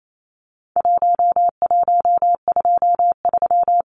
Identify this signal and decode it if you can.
Morse code, heard as a keyed tone, about 28 wpm, 700 Hz. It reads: 1123
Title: Numbers - 4-Digit 28wpm